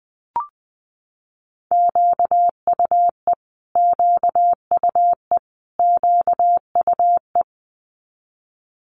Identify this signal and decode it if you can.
Morse code, heard as a keyed tone, about 20 wpm, 700 Hz. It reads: QUE QUE QUE
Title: Multi-letter Phonemes Repeated 3x 20wpm